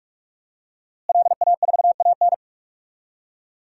Morse code, heard as a keyed tone, about 46 wpm, 700 Hz. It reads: LA4AN
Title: DXCC-100 Call Signs - 5 Characters - Repeated 2x - 45wpm